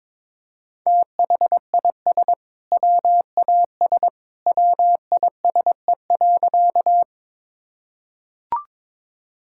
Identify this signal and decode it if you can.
Morse code, heard as a keyed tone, about 22 wpm, 700 Hz. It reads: THIS WAS WISE.
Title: Sentences from Top 1600 Words 22wpm